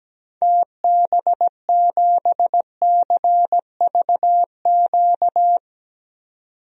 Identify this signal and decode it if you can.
Morse code, heard as a keyed tone, about 17 wpm, 700 Hz. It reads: TB7CVQ
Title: DXCC-100 Call Signs - 6 Characters - 17wpm